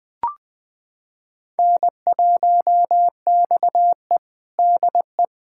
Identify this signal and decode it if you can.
Morse code, heard as a keyed tone, about 20 wpm, 700 Hz. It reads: N1XE DE
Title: USA Callsign DE Callsign - 4 Characters - 20wpm